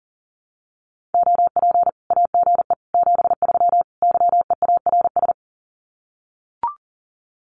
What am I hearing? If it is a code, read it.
Morse code, heard as a keyed tone, about 40 wpm, 700 Hz. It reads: OP AGE 73 YEARS
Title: Common QSO Phrases 40wpm